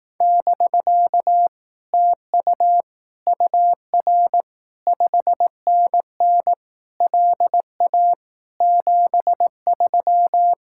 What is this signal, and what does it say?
Morse code, heard as a keyed tone, about 18 wpm, 700 Hz. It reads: <BK> TU UR 5NN LA 73
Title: Speed-Racing Common QSO Phrases Encore - 15wpm